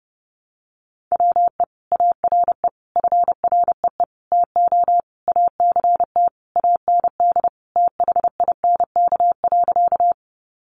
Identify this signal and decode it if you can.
Morse code, heard as a keyed tone, about 30 wpm, 700 Hz. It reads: WE ARE FREE TO ACT AND THINK.
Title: Sentences from Top 500 Words 30wpm